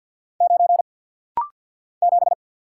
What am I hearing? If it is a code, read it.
Morse code, heard as a keyed tone, about 52 wpm, 700 Hz. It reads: 9 7
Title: Single Letter-Number - Rapid-Fire 50wpm